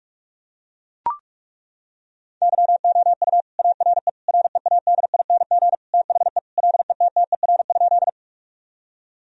Morse code, heard as a keyed tone, about 41 wpm, 700 Hz. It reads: YOU ARE READING THE LETTER?
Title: Sentences from Top 1000 Words - Review Part 3 - 45wpm